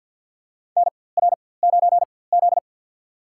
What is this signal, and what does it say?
Morse code, heard as a keyed tone, about 52 wpm, spaced as for 24 wpm, 700 Hz. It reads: NR9Z
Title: USA Call Signs - 4x Character Spacing - 50wpm